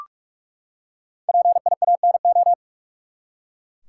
Morse code, heard as a keyed tone, about 45 wpm, 700 Hz. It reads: PIANO
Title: Five Letter Words - 45wpm